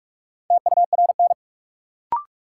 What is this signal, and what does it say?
Morse code, heard as a keyed tone, about 46 wpm, 700 Hz. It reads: TURN
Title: Sentences from Top 200 Words - New Word Drill - 45wpm